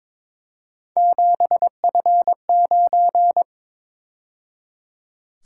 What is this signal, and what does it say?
Morse code, heard as a keyed tone, about 26 wpm, 700 Hz. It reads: 7F9
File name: Speed-Racing - Three Letter-Number Combinations - 22wpm